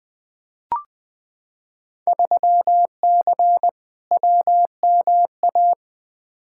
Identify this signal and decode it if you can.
Morse code, heard as a keyed tone, about 20 wpm, 700 Hz. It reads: 3C WMA
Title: Field Day Exchanges Encore - 20wpm